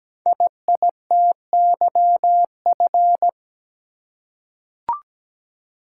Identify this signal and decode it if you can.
Morse code, heard as a keyed tone, about 17 wpm, 700 Hz. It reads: IITYF
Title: Code Groups - 5 Letters - 17wpm